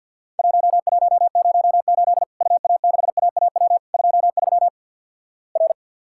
Morse code, heard as a keyed tone, about 50 wpm, 700 Hz. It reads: 1108 UA6AAW 23 R
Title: Worked All Europe DX Contest - QTC Exchange - 50wpm